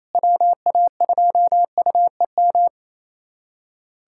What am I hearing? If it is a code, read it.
Morse code, heard as a keyed tone, about 28 wpm, 700 Hz. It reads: WA2UEM
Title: USA Callsign DE Callsign - 6 Characters - 28wpm